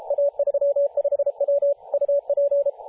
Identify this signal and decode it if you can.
Morse code, heard as a keyed tone, about 33 wpm, 570 Hz, with radio noise. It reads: A35W UP
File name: posts / 2017-jan-27-0632z-a35w-14-031mhz-cw